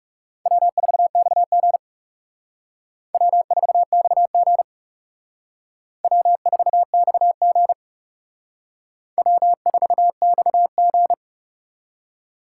Morse code, heard as a keyed tone, about 37 wpm, 700 Hz. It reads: W4XG W4XG W4XG W4XG
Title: Speed Racing - USA 1x2 Call Signs 30wpm